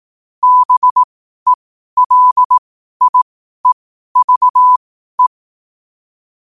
Morse code, heard as a keyed tone, about 18 wpm, spaced as for 12 wpm, 1000 Hz. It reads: BELIEVE
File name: qotdfiles / 12wpm